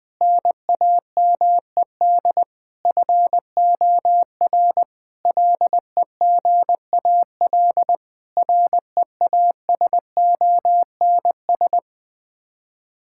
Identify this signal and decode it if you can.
Morse code, heard as a keyed tone, about 20 wpm, 700 Hz. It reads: NAMED FOR LEGAL REASONS
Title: Sets of 8 Words - 20wpm